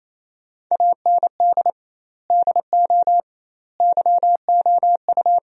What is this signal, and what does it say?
Morse code, heard as a keyed tone, about 28 wpm, 700 Hz. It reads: AND DO YOU
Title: 2x Word-Spacing - Sentences from Top 200 Words - 28wpm